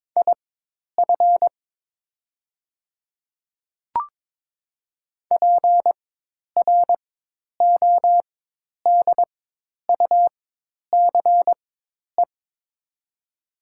Morse code, heard as a keyed tone, about 22 wpm, spaced as for 10 wpm, 700 Hz. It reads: IF PRODUCE
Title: 4x Character Spacing - Top 500 Words - 22wpm